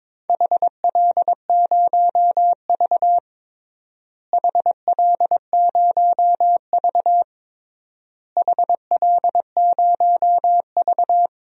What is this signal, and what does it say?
Morse code, heard as a keyed tone, about 22 wpm, 700 Hz. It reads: HL0V HL0V HL0V
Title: DXCC-100 Call Signs - 4 Characters - Repeated 3x - 22wpm